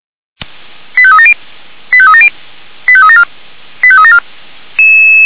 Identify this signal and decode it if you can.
Five-tone selective call (ZVEI), heard as quick runs of five tones, tones of about 70 ms. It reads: 85370 85370 85374 85374
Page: /464/SDR/ZVEI/89155/